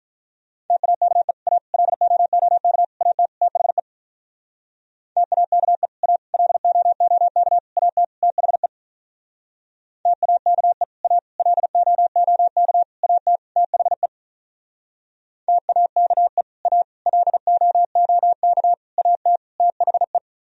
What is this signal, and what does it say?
Morse code, heard as a keyed tone, about 44 wpm, 700 Hz. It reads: TAKE A LOOK AT THE TAKE A LOOK AT THE TAKE A LOOK AT THE TAKE A LOOK AT THE